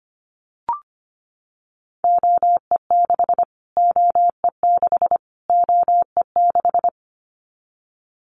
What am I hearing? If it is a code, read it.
Morse code, heard as a keyed tone, about 25 wpm, 700 Hz. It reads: OE6 OE6 OE6
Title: Callsign Prefixes for Top 25 DXCC - Repeated 3x - 25wpm